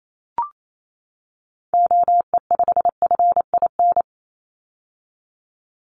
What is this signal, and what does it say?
Morse code, heard as a keyed tone, about 28 wpm, 700 Hz. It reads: OE5FIN
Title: Call Signs - 28wpm